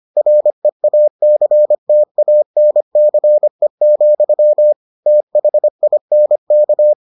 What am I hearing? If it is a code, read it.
Morse code, heard as a keyed tone, about 25 wpm, 600 Hz. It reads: REACTANCE, THINK